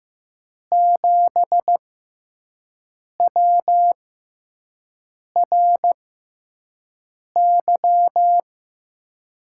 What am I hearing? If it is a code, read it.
Morse code, heard as a keyed tone, about 15 wpm, spaced as for 5 wpm, 700 Hz. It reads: 7WRY
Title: USA Call Signs - 6 Characters - 6x Character Spacing - 15wpm